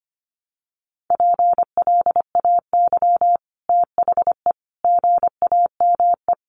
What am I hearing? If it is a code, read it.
Morse code, heard as a keyed tone, about 25 wpm, 700 Hz. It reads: PLAY THE GAME
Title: Sentences from Top 500 Words 25wpm